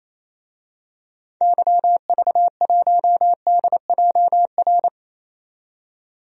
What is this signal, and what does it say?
Morse code, heard as a keyed tone, about 28 wpm, 700 Hz. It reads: YV1DJR
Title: DXCC-100 Call Signs - 6 Characters - 28wpm